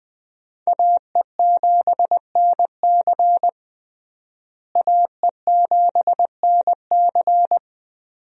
Morse code, heard as a keyed tone, about 20 wpm, 700 Hz. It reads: AE7NC AE7NC
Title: USA Call Signs - 5 Characters - Repeated 2x - 20wpm